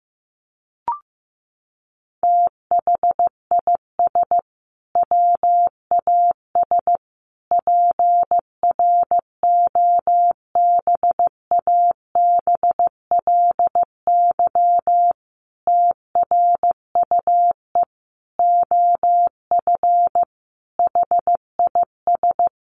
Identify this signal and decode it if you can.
Morse code, heard as a keyed tone, about 15 wpm, 700 Hz. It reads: THIS WAS PROBABLY TRUE OF HIS
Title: Sentences from Top 700 Words 15wpm